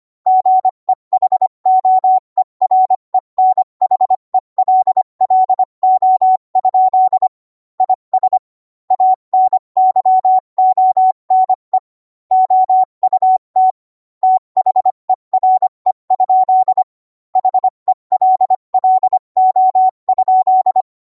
Morse code, heard as a keyed tone, about 25 wpm, 750 Hz. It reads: GEHOEREN HELLO? IS ANYONE OUT THERE? HELLO?